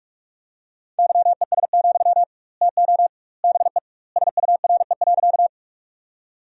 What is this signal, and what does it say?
Morse code, heard as a keyed tone, about 46 wpm, 700 Hz. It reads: YES, TO BE SURE.
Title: 2x Word-Spacing - Sentences from Top 500 Words - 45wpm